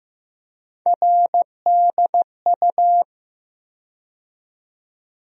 Morse code, heard as a keyed tone, about 15 wpm, 700 Hz. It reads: RDU
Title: USA Call Sign Postfixes - 3 Letters 15wpm